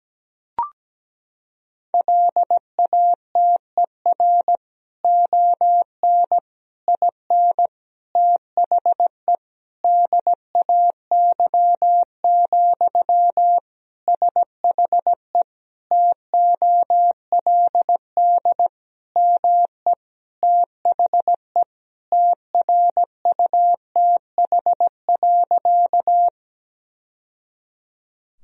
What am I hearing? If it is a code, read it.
Morse code, heard as a keyed tone, about 17 wpm, 700 Hz. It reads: LATER ON IN THE DAY, SHE TOLD ME THE TRUTH.
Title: Sentences from Top 1000 Words - Review Part 4 - 17wpm